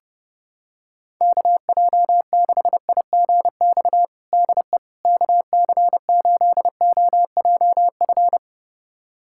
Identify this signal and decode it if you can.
Morse code, heard as a keyed tone, about 29 wpm, 700 Hz. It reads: KJ6IGX DE KC8OJF
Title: USA Callsign DE Callsign - 6 Characters - 30wpm